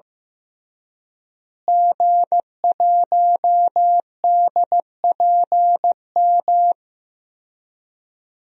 Morse code, heard as a keyed tone, about 16 wpm, 700 Hz. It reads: G1DPM
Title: DXCC-100 Call Signs - 5 Characters - Repeated 3x - 15wpm